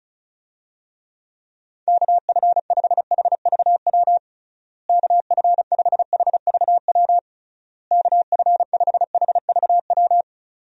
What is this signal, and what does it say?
Morse code, heard as a keyed tone, about 36 wpm, 700 Hz. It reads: KF5HVW KF5HVW KF5HVW
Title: USA Call Signs - 6 Characters - Repeated 3x - 35wpm